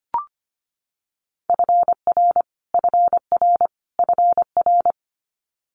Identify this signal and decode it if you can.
Morse code, heard as a keyed tone, about 25 wpm, 700 Hz. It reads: FR FR FR
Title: Multi-letter Phonemes Repeated 3x 25wpm